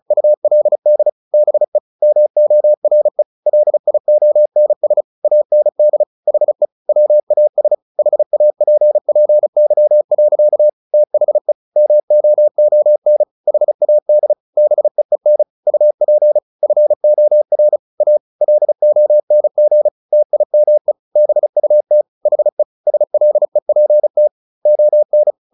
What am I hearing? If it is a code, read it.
Morse code, heard as a keyed tone, about 34 wpm, 600 Hz. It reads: ULD BE MORE LIONS AND HE WAS HAPPY. THE MOON HAD BEEN UP FOR A LONG TIME BUT HE SLEPT ON
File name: the old man and the sea 2 35wpm